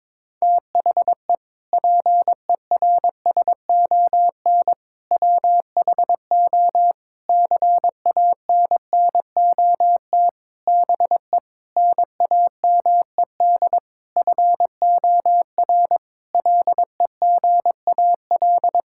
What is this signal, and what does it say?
Morse code, heard as a keyed tone, about 22 wpm, 700 Hz. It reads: THE PERSON WHO CANNOT BE NAMED FOR LEGAL
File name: practice / 22wpm / Sentences with 8-Word Phrases - 22wpm